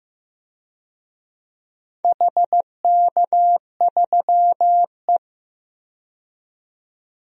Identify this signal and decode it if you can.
Morse code, heard as a keyed tone, about 15 wpm, 700 Hz. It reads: HK3E
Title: DXCC-100 Call Signs - 4 Characters - 15wpm